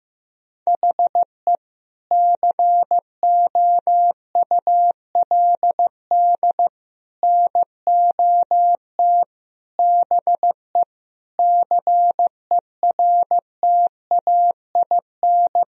Morse code, heard as a keyed tone, about 15 wpm, 700 Hz. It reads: HE COULD NOT BE CERTAIN
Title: Sentences from Top 1000 Words - Review Part 2 - 15wpm